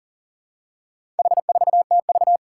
Morse code, heard as a keyed tone, about 37 wpm, 700 Hz. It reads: H4TV
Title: Four Challenging Letter-Number Combinations 40wpm